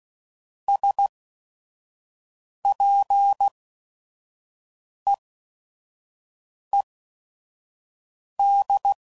Morse code, heard as a keyed tone, about 16 wpm, spaced as for 5 wpm, 800 Hz. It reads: SPEED
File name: CW Code Groups @5WPM / Weather Sets / Weather 5 5